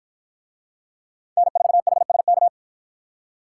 Morse code, heard as a keyed tone, about 53 wpm, 700 Hz. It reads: N4HSK